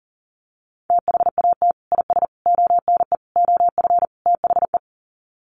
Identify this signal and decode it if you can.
Morse code, heard as a keyed tone, about 40 wpm, 700 Hz. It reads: THAT IS ONE OF THE
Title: Sets of 5 Words Encore 40wpm